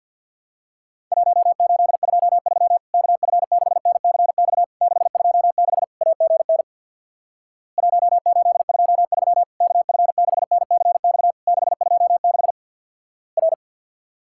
Morse code, heard as a keyed tone, about 50 wpm, 700 Hz. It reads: 1823 KF6NCX 626 AGN 1823 KF6NCX 626 R
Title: Worked All Europe DX Contest - QTC Exchange with Repeats - 50wpm